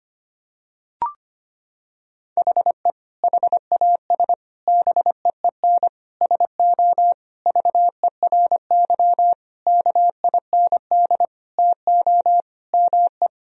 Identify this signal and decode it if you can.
Morse code, heard as a keyed tone, about 25 wpm, 700 Hz. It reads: HE HAS BEEN SO VERY KIND TO ME
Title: Sentences from Top 1000 Words - Review Part 1 - 25wpm